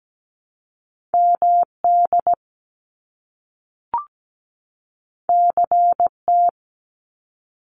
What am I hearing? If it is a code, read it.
Morse code, heard as a keyed tone, about 17 wpm, 700 Hz. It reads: MD CT
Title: US States 17wpm